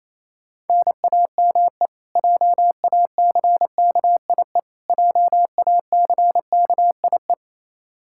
Morse code, heard as a keyed tone, about 28 wpm, 700 Hz. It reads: NAME JACKIE JACKIE
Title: Practice QSO - Exchange 1 - Beginning - 28wpm